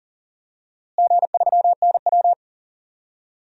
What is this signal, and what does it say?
Morse code, heard as a keyed tone, about 40 wpm, 700 Hz. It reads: G3NW
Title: DXCC-100 Call Signs - 4 Characters - Repeated 3x - 40wpm